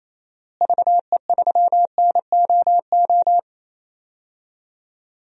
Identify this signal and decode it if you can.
Morse code, heard as a keyed tone, about 28 wpm, 700 Hz. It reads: VE3NOO
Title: DXCC-100 Call Signs - 6 Characters - Repeated 2x - 28wpm